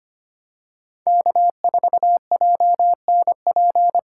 Morse code, heard as a keyed tone, about 25 wpm, 700 Hz. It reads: K4JNP
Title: USA Call Signs - 8x Character Spacing - 25wpm